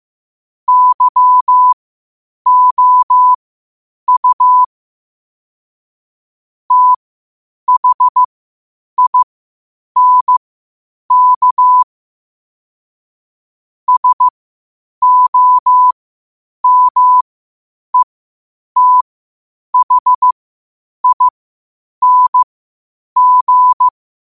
Morse code, heard as a keyed tone, about 15 wpm, spaced as for 8 wpm, 1000 Hz. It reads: YOU THINK SOMETHING
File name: qotdfiles / 8wpm